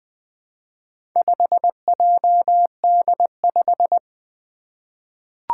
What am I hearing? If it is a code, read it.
Morse code, heard as a keyed tone, about 21 wpm, 700 Hz. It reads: 5JD5
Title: Four Challenging Letter-Number Combinations 20wpm